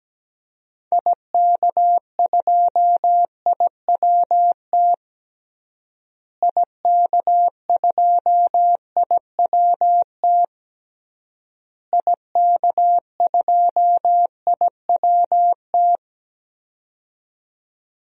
Morse code, heard as a keyed tone, about 17 wpm, 700 Hz. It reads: IK2IWT IK2IWT IK2IWT
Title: DXCC-100 Call Signs - 6 Characters - Repeated 3x - 17wpm